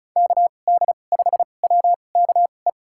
Morse code, heard as a keyed tone, about 34 wpm, spaced as for 25 wpm, 700 Hz. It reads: KD5WKE
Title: USA Call Signs - 2x Character Spacing - 35wpm